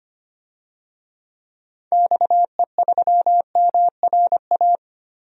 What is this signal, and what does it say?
Morse code, heard as a keyed tone, about 25 wpm, 690 Hz. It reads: XE3MRA
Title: DXCC-100 Call Signs - 6 Characters - 25wpm